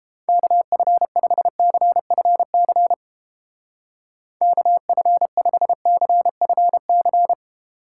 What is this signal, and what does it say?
Morse code, heard as a keyed tone, about 31 wpm, 700 Hz. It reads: KF5CFC KF5CFC